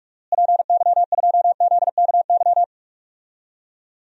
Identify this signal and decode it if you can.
Morse code, heard as a keyed tone, about 46 wpm, 700 Hz. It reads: PY2ZKY
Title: DXCC-100 Call Signs - 6 Characters - Repeated 3x - 45wpm